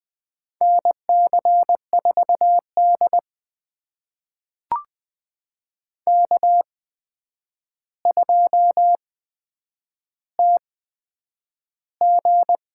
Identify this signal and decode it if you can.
Morse code, heard as a keyed tone, about 20 wpm, spaced as for 9 wpm, 700 Hz. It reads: NC4D K2TG
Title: USA Call Signs - 8x Character Spacing - 20wpm